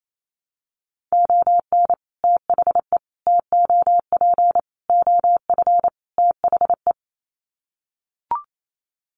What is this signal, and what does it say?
Morse code, heard as a keyed tone, about 28 wpm, 700 Hz. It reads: ON THE TOP OF THE
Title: Sets of 5 Words Encore 28wpm